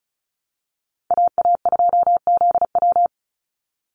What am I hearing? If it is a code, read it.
Morse code, heard as a keyed tone, about 34 wpm, 700 Hz. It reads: AA2ZW
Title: Call Signs Repeated 2x - 35wpm